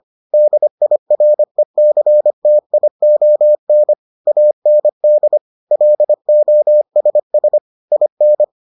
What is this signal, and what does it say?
Morse code, heard as a keyed tone, about 25 wpm, 600 Hz. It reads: DIRECTION AND LOSS IN